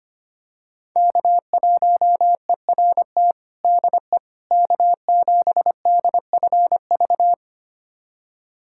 Morse code, heard as a keyed tone, about 25 wpm, 700 Hz. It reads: K1ERT DE K7DFV
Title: USA Callsign DE Callsign - 5 Characters - 25wpm